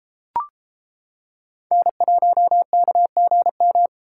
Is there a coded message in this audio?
Yes: Morse code from 1.7 to 3.9 s, heard as a keyed tone, about 28 wpm, 700 Hz. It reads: N1KGM